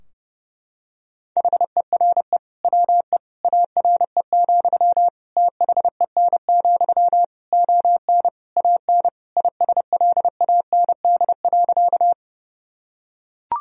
Morse code, heard as a keyed tone, about 30 wpm, 700 Hz. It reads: HERE WE ARE, THEN, ON AN ISLAND.
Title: Sentences from Top 1000 Words - Review Part 3 - 30wpm